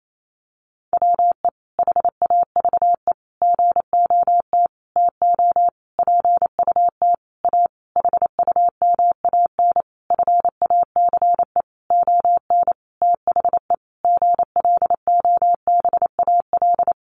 Morse code, heard as a keyed tone, about 28 wpm, 700 Hz. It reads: WE HAVE GOT TO PUT A HUMAN FACE ON THE GLOBAL
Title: Sentences from Top 1200 Words 28wpm